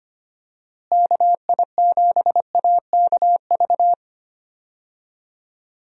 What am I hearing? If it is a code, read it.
Morse code, heard as a keyed tone, about 25 wpm, 700 Hz. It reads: KI7AKV
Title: USA Call Signs - 6 Characters - Repeated 2x - 25wpm